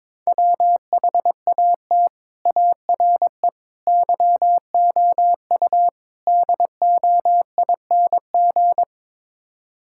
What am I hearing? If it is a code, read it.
Morse code, heard as a keyed tone, about 21 wpm, 700 Hz. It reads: WHAT ARE YOU DOING
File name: practice / 22wpm / Build A Phrase - 5-Word Phrases - 22wpm